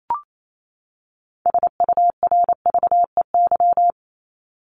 Morse code, heard as a keyed tone, about 28 wpm, 700 Hz. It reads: SURVEY
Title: Top 1501-1750 Words 28wpm